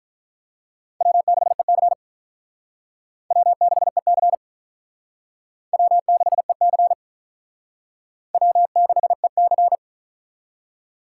Morse code, heard as a keyed tone, about 45 wpm, 700 Hz. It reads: W6EC W6EC W6EC W6EC